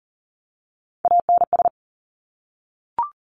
Morse code, heard as a keyed tone, about 42 wpm, 700 Hz. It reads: ANS
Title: Top Three Letter Combinations in Words 40wpm